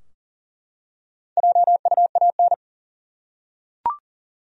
Morse code, heard as a keyed tone, about 40 wpm, 700 Hz. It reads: JUAN
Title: USA Common Names - Max 5 Letters - 40wpm